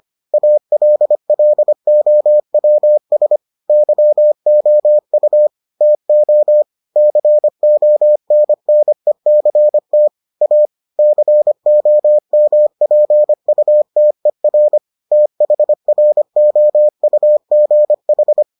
Morse code, heard as a keyed tone, about 25 wpm, 600 Hz. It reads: ALLOWS YOU TO CONNECT A COMPUTER THROUGH